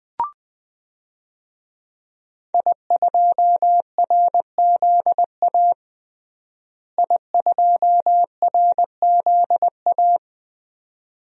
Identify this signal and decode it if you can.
Morse code, heard as a keyed tone, about 20 wpm, 700 Hz. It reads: I2RZA I2RZA
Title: DXCC-100 Call Signs - 5 Characters - Repeated 2x - 20wpm